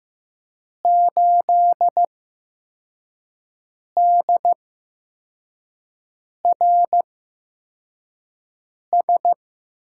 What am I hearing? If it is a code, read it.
Morse code, heard as a keyed tone, about 15 wpm, spaced as for 4 wpm, 700 Hz. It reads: 8DRS
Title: USA Call Signs - 5 Characters - 8x Character Spacing - 15wpm